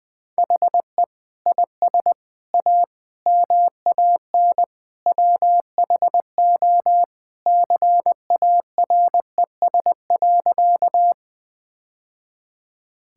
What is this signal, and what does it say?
Morse code, heard as a keyed tone, about 20 wpm, 700 Hz. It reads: HE IS A MAN WHO CARES.
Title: Sentences with 3-Word Phrases - 20wpm